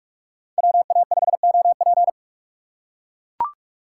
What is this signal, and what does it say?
Morse code, heard as a keyed tone, about 44 wpm, 700 Hz. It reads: WA5OP